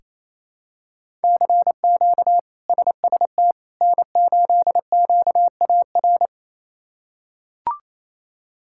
Morse code, heard as a keyed tone, about 28 wpm, 700 Hz. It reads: CQ SST N8QAR
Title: K1USN SST QSO Practice - 28wpm